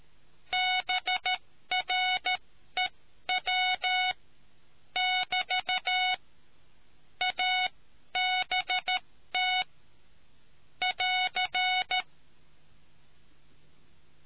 Morse code, heard as a keyed tone, about 13 wpm, spaced as for 11 wpm, 740 Hz. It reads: BREW <BT> ABT <AR>